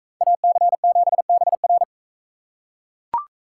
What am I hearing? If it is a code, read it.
Morse code, heard as a keyed tone, about 40 wpm, 700 Hz. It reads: AC7BR